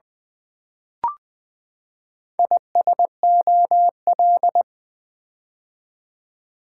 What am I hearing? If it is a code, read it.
Morse code, heard as a keyed tone, about 20 wpm, 700 Hz. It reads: ISOL
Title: Build A Word - 4 Letters At A Time - 8-Letter Words - 20wpm